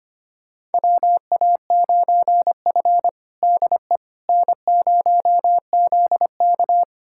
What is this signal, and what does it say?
Morse code, heard as a keyed tone, about 25 wpm, 700 Hz. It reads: WA9F DE N0ZK
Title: USA Callsign DE Callsign - 4 Characters - 25wpm